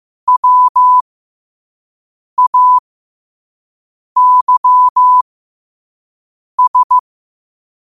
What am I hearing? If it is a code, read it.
Morse code, heard as a keyed tone, about 15 wpm, spaced as for 5 wpm, 1000 Hz. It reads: WAYS
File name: qotdfiles / 5wpm